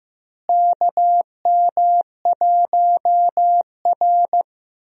Morse code, heard as a keyed tone, about 15 wpm, 700 Hz. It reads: KM1R
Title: USA Call Signs - 4 Characters - Repeated 2x - 15wpm